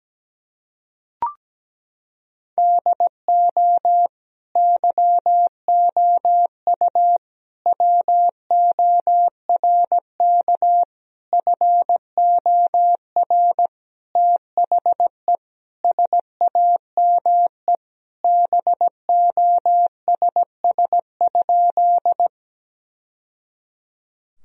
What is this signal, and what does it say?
Morse code, heard as a keyed tone, about 17 wpm, 700 Hz. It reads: DO YOU WORK FOR THE SAME BOSS?
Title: Sentences from Top 1000 Words - Review Part 4 - 17wpm